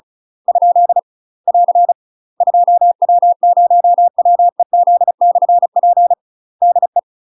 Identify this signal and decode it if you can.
Morse code, heard as a keyed tone, about 36 wpm, 700 Hz. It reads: ? <AR> 2W0WEZ/P DE
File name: UK MASTER QSOs TRAINING / 35WPM / UK MASTER QSOs 0028